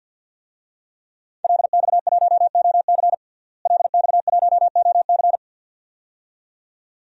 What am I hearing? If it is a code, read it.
Morse code, heard as a keyed tone, about 46 wpm, 700 Hz. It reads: LX1OC LX1OC
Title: DXCC-100 Call Signs - 5 Characters - Repeated 2x - 50wpm